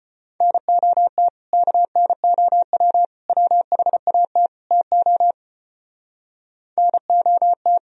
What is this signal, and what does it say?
Morse code, heard as a keyed tone, about 36 wpm, 700 Hz. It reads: NOT KNOW WHAT TO NOT